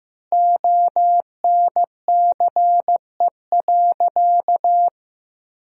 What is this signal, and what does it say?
Morse code, heard as a keyed tone, about 15 wpm, 700 Hz. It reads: ONCE.
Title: 2x Word-Spacing - Sentences from Top 500 Words - 15wpm